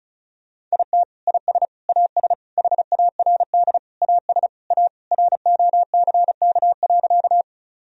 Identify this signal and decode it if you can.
Morse code, heard as a keyed tone, about 35 wpm, 700 Hz. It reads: IT IS AS HARD AS A ROCK.
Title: Sentences from Top 1000 Words - Review Part 2 - 35wpm